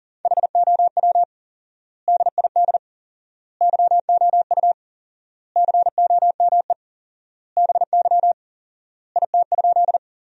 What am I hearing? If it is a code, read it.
Morse code, heard as a keyed tone, about 40 wpm, 700 Hz. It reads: HOW DID YOU COME BY IT?
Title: 4x Word-Spacing - Sentences from Top 100 Words - 40wpm